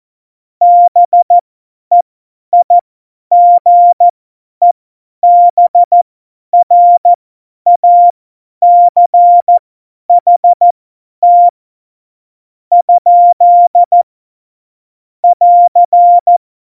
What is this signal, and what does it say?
Morse code, heard as a keyed tone, about 14 wpm, spaced as for 10 wpm, 700 Hz. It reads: BEIGEBRACHT ? <AR>
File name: Morsekurs / Witze / Witz 10 April 2018, 50 F70